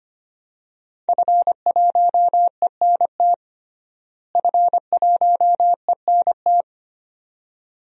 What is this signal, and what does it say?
Morse code, heard as a keyed tone, about 25 wpm, 700 Hz. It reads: F1ENT F1ENT
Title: DXCC-100 Call Signs - 5 Characters - Repeated 2x - 25wpm